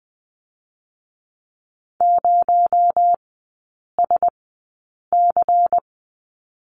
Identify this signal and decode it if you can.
Morse code, heard as a keyed tone, about 20 wpm, spaced as for 8 wpm, 700 Hz. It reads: 0SC
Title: 218 Lesson 35e - Number 0 - Three Characters 20wpm